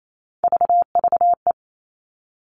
Morse code, heard as a keyed tone, about 28 wpm, 700 Hz. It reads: VVE
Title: All Cut Numbers - 3-Digit 28wpm